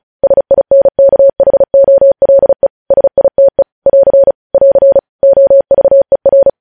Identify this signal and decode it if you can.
Morse code, heard as a keyed tone, about 36 wpm, 550 Hz. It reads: SINKHOLE SITE <AR> <AR> OVER
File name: podcast / News headlines-1x-35x35